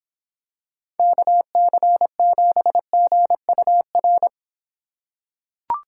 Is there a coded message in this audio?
Yes: Morse code from 1.0 to 4.3 s, heard as a keyed tone, about 23 wpm, 700 Hz. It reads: KC7GUR